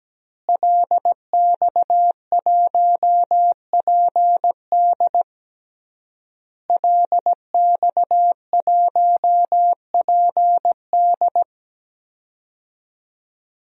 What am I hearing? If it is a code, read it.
Morse code, heard as a keyed tone, about 17 wpm, 700 Hz. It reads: LX1PD LX1PD
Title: DXCC-100 Call Signs - 5 Characters - Repeated 3x - 17wpm